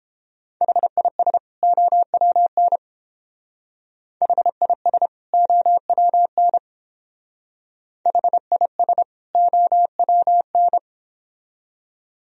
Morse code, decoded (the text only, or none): HIS OWN HIS OWN HIS OWN